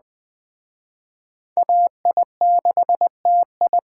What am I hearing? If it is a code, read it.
Morse code, heard as a keyed tone, about 20 wpm, 700 Hz. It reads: AI6TI
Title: Speed Racing - USA Call Signs 17wpm